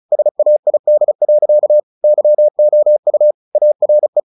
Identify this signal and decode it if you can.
Morse code, heard as a keyed tone, about 37 wpm, 600 Hz. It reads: SAID. YOU ARE
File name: the old man and the sea / the old man and the sea 1 35wpm